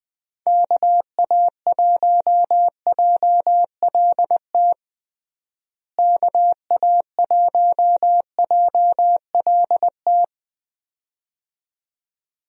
Morse code, heard as a keyed tone, about 20 wpm, 700 Hz. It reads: KA1JLT KA1JLT
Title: USA Call Signs - 6 Characters - Repeated 2x - 20wpm